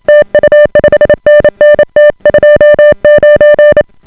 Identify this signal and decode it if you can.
Morse code, heard as a keyed tone, about 28 wpm, 590 Hz. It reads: TU 5NN T29